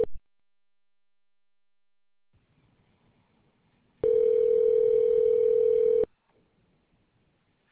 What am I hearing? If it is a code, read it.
Ringback tone (North America), 440 + 480 Hz.